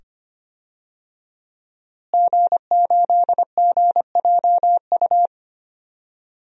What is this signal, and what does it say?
Morse code, heard as a keyed tone, about 25 wpm, 700 Hz. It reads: G8GJU
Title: DXCC-100 Call Signs - 5 Characters - Repeated 2x - 25wpm